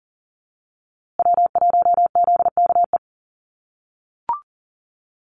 Morse code, heard as a keyed tone, about 40 wpm, 700 Hz. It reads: W1ZKE
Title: USA Call Signs Repeated 3x 40wpm